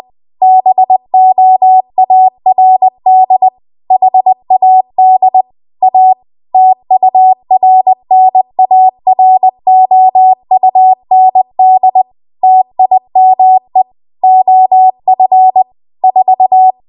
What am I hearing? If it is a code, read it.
Morse code, heard as a keyed tone, about 20 wpm, 750 Hz. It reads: BOARD HAD A TURNAROUND TIME OF 4
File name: Archive / 20 WPM / 20WPM